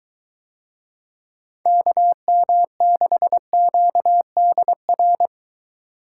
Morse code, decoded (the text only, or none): KM6QDR